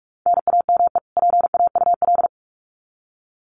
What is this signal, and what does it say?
Morse code, heard as a keyed tone, about 44 wpm, 700 Hz. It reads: NAME PAUL
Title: Common QSO Phrases 45wpm